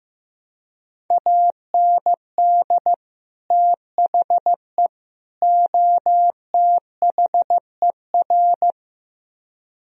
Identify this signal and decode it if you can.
Morse code, heard as a keyed tone, about 16 wpm, 700 Hz. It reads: AND THE OTHER
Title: Speed-Racing - Sets of 3 Words 15wpm